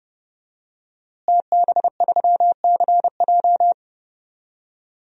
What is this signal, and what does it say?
Morse code, heard as a keyed tone, about 32 wpm, 700 Hz. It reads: TB3CJ
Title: DXCC-100 Call Signs - 5 Characters - Repeated 2x - 30wpm